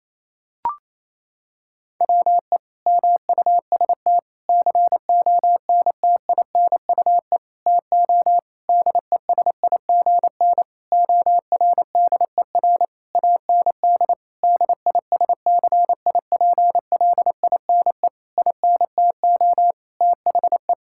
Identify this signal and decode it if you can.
Morse code, heard as a keyed tone, about 28 wpm, 700 Hz. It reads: WE MUST CONTINUE TO DESIGN ORDER AND DISCIPLINE INTO THE
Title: Sentences from Top 1000 Words - Review Part 6 - 28wpm